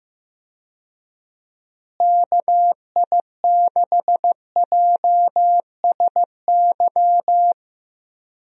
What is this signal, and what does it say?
Morse code, heard as a keyed tone, about 16 wpm, 700 Hz. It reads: KI6JSY
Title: USA Call Signs - 6 Characters - Repeated 2x - 15wpm